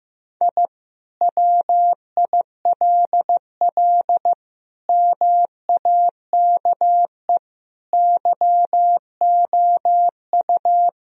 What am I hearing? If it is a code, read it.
Morse code, heard as a keyed tone, about 15 wpm, 700 Hz. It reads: I WILL MAKE YOU
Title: Sentences from Top 1000 Words - Review Part 4 - 15wpm